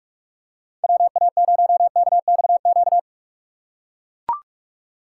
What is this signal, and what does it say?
Morse code, heard as a keyed tone, about 43 wpm, 700 Hz. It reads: WA0KXQ
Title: USA Call Signs - 6 Characters - 3x Character Spacing - 45wpm